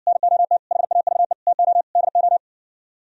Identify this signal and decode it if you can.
Morse code, heard as a keyed tone, about 60 wpm, 700 Hz. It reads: NOT HAVE TO DO